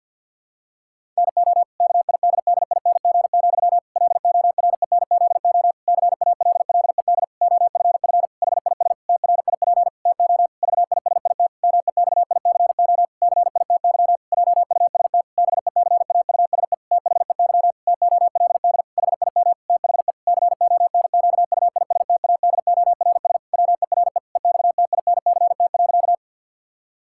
Morse code, heard as a keyed tone, about 52 wpm, 700 Hz. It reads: NO KIDDING, LORENZO CALLED OFF HIS TRIP TO VISIT MEXICO CITY JUST BECAUSE THEY TOLD HIM THE CONQUISTADORS WERE EXTINCT.
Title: Pangram Challenge v2 - 50wpm